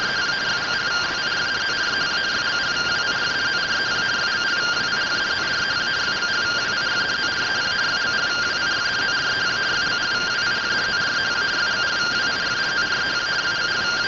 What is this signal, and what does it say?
Two-tone teleprinter signal, about 170 Hz shift.
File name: findings / AXIAL fr1 3mm.gen-34 rtty rx